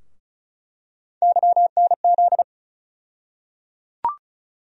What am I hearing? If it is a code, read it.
Morse code, heard as a keyed tone, about 36 wpm, 700 Hz. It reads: YNZ
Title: USA Call Sign Postfixes - 3 Letters 35wpm